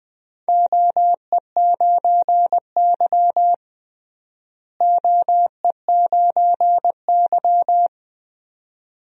Morse code, heard as a keyed tone, about 20 wpm, 700 Hz. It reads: OE9Y OE9Y
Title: DXCC-100 Call Signs - 4 Characters - Repeated 2x - 20wpm